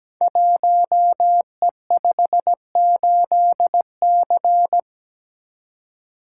Morse code, heard as a keyed tone, about 17 wpm, 700 Hz. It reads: 1E58C
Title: Code Groups - 5 Letter:Numbers - 17wpm